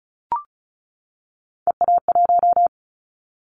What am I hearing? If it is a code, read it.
Morse code, heard as a keyed tone, about 34 wpm, 700 Hz. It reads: EA1
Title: Callsign Prefixes for Top 25 DXCC - 35wpm